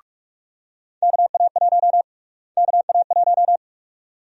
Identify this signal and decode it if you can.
Morse code, heard as a keyed tone, about 45 wpm, 700 Hz. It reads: KA1 KA1
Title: USA Call Sign Prefixes - Repeated 2x - 45wpm